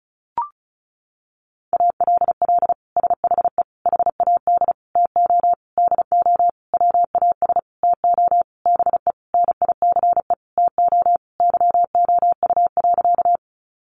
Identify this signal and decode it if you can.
Morse code, heard as a keyed tone, about 34 wpm, 700 Hz. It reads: ALL SHE HAD TO DO WAS TO BE NICE TO YOU.
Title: Sentences from Top 1100 Words 35wpm